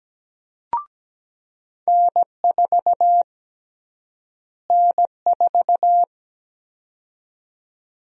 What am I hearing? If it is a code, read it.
Morse code, heard as a keyed tone, about 17 wpm, 700 Hz. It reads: N4 N4
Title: USA Call Sign Prefixes - Repeated 2x - 17wpm